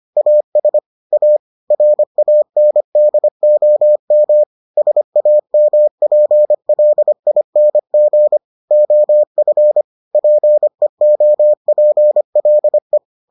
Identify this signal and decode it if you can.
Morse code, heard as a keyed tone, about 25 wpm, 600 Hz. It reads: AS A RANDOM SAMPLING OF PEOPLE